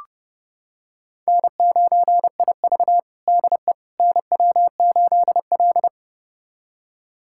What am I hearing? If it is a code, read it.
Morse code, heard as a keyed tone, about 30 wpm, 700 Hz. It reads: N9IV DE NW8L
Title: USA Callsign DE Callsign - 4 Characters - 30wpm